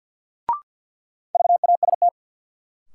Morse code, heard as a keyed tone, about 53 wpm, 700 Hz. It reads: VAST
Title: Sentences from Top 1300 Words - New Word Drill Alt - 50wpm